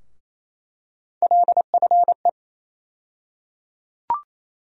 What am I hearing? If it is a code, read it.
Morse code, heard as a keyed tone, about 28 wpm, 700 Hz. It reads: LFE
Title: USA Call Sign Postfixes - 3 Letters 28wpm